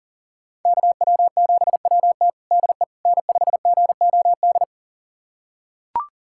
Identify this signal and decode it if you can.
Morse code, heard as a keyed tone, about 40 wpm, 700 Hz. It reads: KW7WT DE N5GOD
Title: USA Callsign DE Callsign - 5 Characters - 40wpm